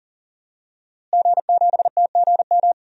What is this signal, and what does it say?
Morse code, heard as a keyed tone, about 40 wpm, 700 Hz. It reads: G7TGM
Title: DXCC-100 Call Signs - 5 Characters - Repeated 2x - 40wpm